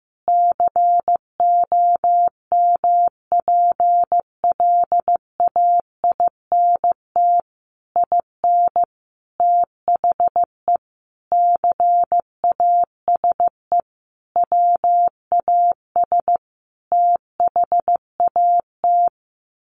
Morse code, heard as a keyed tone, about 15 wpm, 700 Hz. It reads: COMPLAINT IN THE CASE WAS THAT
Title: Sentences from Top 1400 Words 15wpm